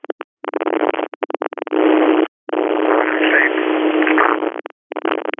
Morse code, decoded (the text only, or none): none